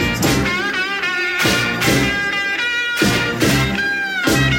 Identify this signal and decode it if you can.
Emergency Alert System (SAME) tones, heard as harsh buzzing bursts. It is heard from 0.5 to 4.2 s.
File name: wrct-log hi- 16